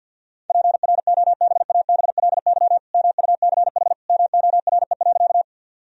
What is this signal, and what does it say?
Morse code, heard as a keyed tone, about 50 wpm, 700 Hz. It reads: PROBABLY MUCH MORE.
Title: Sentences from Top 1000 Words - Review Part 4 - 50wpm